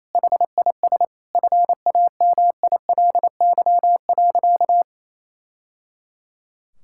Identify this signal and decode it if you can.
Morse code, heard as a keyed tone, about 27 wpm, 700 Hz. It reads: HIS FAMILY.
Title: Sentences from Top 1000 Words - Review Part 6 - 28wpm